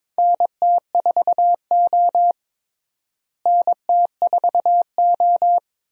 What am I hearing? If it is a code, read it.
Morse code, heard as a keyed tone, about 21 wpm, 700 Hz. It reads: NT4O NT4O
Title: USA Call Signs - 4 Characters - Repeated 2x - 22wpm